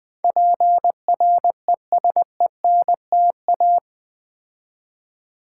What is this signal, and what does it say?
Morse code, heard as a keyed tone, about 20 wpm, 700 Hz. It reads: PRESENTA
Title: Build A Word - 4 Letters At A Time - 12-Letter Words - 20wpm